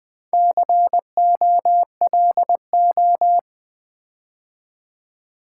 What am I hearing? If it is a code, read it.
Morse code, heard as a keyed tone, about 20 wpm, 700 Hz. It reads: COLO
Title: Build A Word - 4 Letters At A Time - 8-Letter Words - 20wpm